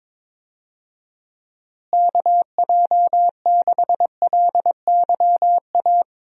Morse code, heard as a keyed tone, about 21 wpm, 700 Hz. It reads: KJ6LYA
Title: USA Call Signs - 6 Characters - Repeated 2x - 22wpm